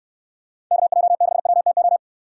Morse code, heard as a keyed tone, about 69 wpm, 700 Hz. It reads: KJ6WTO